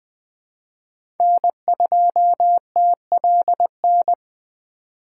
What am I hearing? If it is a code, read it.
Morse code, heard as a keyed tone, about 20 wpm, 700 Hz. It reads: N2TLN
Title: USA Call Signs - 5 Characters - 2x Character Spacing - 20wpm